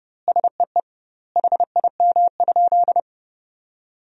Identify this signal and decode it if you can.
Morse code, heard as a keyed tone, about 32 wpm, 700 Hz. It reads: SEE HIM?
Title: 2x Word-Spacing - Sentences from Top 100 Words - 30wpm